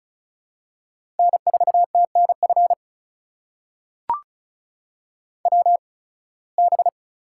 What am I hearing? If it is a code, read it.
Morse code, heard as a keyed tone, about 36 wpm, spaced as for 23 wpm, 700 Hz. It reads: N4TNF WB
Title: USA Call Signs - 8x Character Spacing - 35wpm